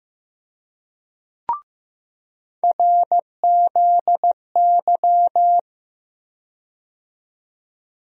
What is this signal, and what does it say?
Morse code, heard as a keyed tone, about 15 wpm, 700 Hz. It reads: RZY
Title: USA Call Sign Postfixes - 3 Letters 15wpm